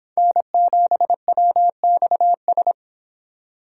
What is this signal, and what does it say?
Morse code, heard as a keyed tone, about 26 wpm, 700 Hz. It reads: N7WXS